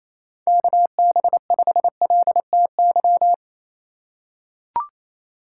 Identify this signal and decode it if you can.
Morse code, heard as a keyed tone, about 27 wpm, 700 Hz. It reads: KB5LTY
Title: USA Call Signs - 6 Characters - 4x Character Spacing - 28wpm